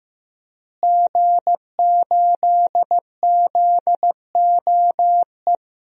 Morse code, heard as a keyed tone, about 16 wpm, 700 Hz. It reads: G8ZOE
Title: DXCC-100 Call Signs - 5 Characters - Repeated 2x - 15wpm